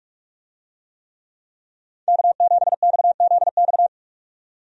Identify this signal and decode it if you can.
Morse code, heard as a keyed tone, about 46 wpm, 700 Hz. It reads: K7XZX
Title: USA Call Signs - 5 Characters - 45wpm